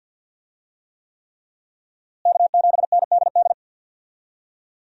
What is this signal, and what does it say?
Morse code, heard as a keyed tone, about 50 wpm, 700 Hz. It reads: K7NDD
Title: USA Call Signs - 5 Characters - Repeated 3x - 50wpm